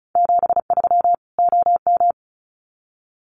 Morse code, heard as a keyed tone, about 34 wpm, 700 Hz. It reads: 73 OM
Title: Common QSO Phrases 35wpm